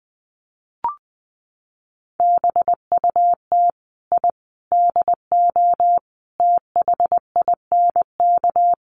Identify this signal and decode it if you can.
Morse code, heard as a keyed tone, about 20 wpm, 700 Hz. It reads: BUT I DO THINK
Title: Sets of 5 Words Encore2 20wpm